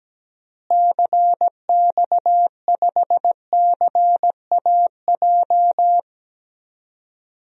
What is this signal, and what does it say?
Morse code, heard as a keyed tone, about 17 wpm, 700 Hz. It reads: CX5CAJ
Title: DXCC-100 Call Signs - 6 Characters - Repeated 2x - 17wpm